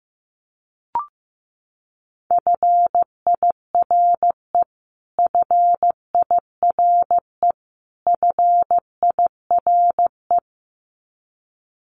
Morse code, heard as a keyed tone, about 15 wpm, 700 Hz. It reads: FIRE FIRE FIRE
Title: Top 500 Words Repeated 15wpm